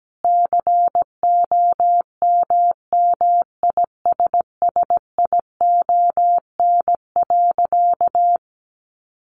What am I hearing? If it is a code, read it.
Morse code, heard as a keyed tone, about 17 wpm, 700 Hz. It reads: COMMISSION.
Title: Sentences from Top 1400 Words 17wpm